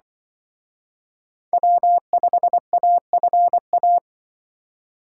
Morse code, heard as a keyed tone, about 23 wpm, 700 Hz. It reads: W5AFA
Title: Speed Racing - USA 1x3 Call Signs 20wpm